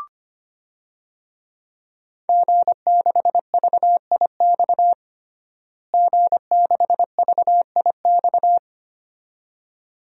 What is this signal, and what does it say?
Morse code, heard as a keyed tone, about 25 wpm, 700 Hz. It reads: G6VIX G6VIX
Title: DXCC-100 Call Signs - 5 Characters - Repeated 2x - 25wpm